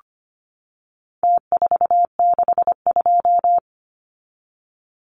Morse code, heard as a keyed tone, about 25 wpm, 700 Hz. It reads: T462
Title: Numbers - Contest Style - 4-Digit 25wpm